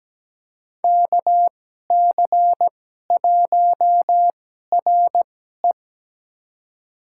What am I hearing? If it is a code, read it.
Morse code, heard as a keyed tone, about 17 wpm, spaced as for 12 wpm, 700 Hz. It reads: KC1RE
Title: USA Call Signs - 5 Characters - 2x Character Spacing - 17wpm